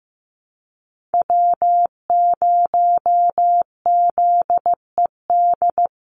Morse code, heard as a keyed tone, about 15 wpm, 700 Hz. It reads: W0ZED
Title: USA Call Signs 15wpm